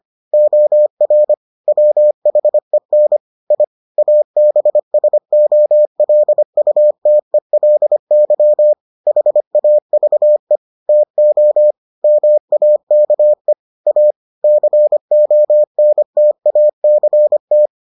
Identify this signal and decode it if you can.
Morse code, heard as a keyed tone, about 25 wpm, 600 Hz. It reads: OR WHEN I ABSOLUTELY HAVE TO MAKE A CONTACT